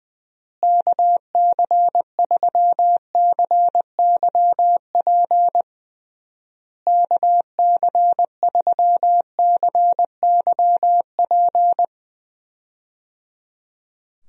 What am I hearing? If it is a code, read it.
Morse code, heard as a keyed tone, about 20 wpm, 700 Hz. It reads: KC3CYP KC3CYP
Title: USA Call Signs - 6 Characters - Repeated 3x - 20wpm